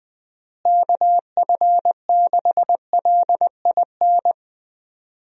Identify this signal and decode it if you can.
Morse code, heard as a keyed tone, about 21 wpm, 700 Hz. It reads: KF6LIN
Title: USA Call Signs - 6 Characters - Repeated 2x - 20wpm